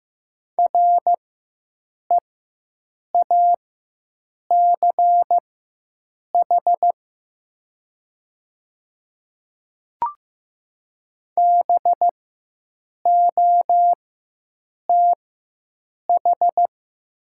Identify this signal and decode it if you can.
Morse code, heard as a keyed tone, about 15 wpm, spaced as for 7 wpm, 690 Hz. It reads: REACH BOTH
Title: 4x Character Spacing - Top 500 Words - 15wpm